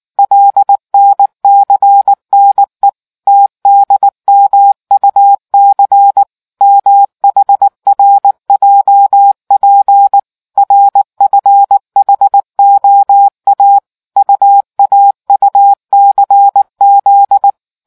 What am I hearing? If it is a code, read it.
Morse code, heard as a keyed tone, about 19 wpm, 800 Hz. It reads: LNCNE TDMUC MHRJP RFHOA UAUCZ